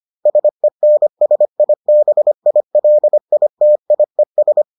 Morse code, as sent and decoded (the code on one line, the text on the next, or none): ... -. ... .. -... .. .-.. .. - .. ...
SENSIBILITIES